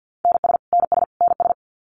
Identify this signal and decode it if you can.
Morse code, heard as a keyed tone, about 50 wpm, 700 Hz. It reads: NS NS NS
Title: Canadian Provinces and Territories - Repeated 3x - 50wpm